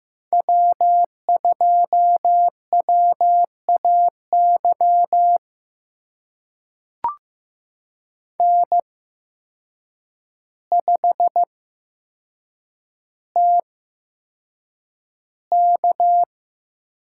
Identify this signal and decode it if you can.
Morse code, heard as a keyed tone, about 15 wpm, spaced as for 7 wpm, 700 Hz. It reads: W2WAY N5TK
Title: USA Call Signs - 8x Character Spacing - 15wpm